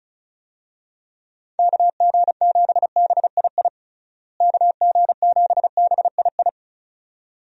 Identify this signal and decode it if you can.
Morse code, heard as a keyed tone, about 36 wpm, 700 Hz. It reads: KG7BII KG7BII
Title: USA Call Signs - 6 Characters - Repeated 2x - 35wpm